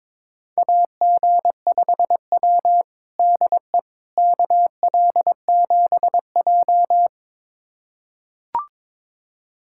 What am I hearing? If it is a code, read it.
Morse code, heard as a keyed tone, about 22 wpm, 700 Hz. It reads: AG5W DE KL7J
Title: USA Callsign DE Callsign - 4 Characters - 22wpm